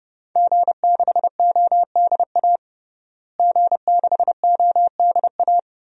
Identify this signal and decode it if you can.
Morse code, heard as a keyed tone, about 30 wpm, 700 Hz. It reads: G6ODA G6ODA
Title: DXCC-100 Call Signs - 5 Characters - Repeated 2x - 30wpm